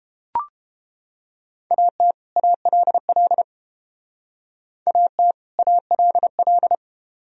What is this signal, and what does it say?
Morse code, heard as a keyed tone, about 30 wpm, 700 Hz. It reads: AT ALL AT ALL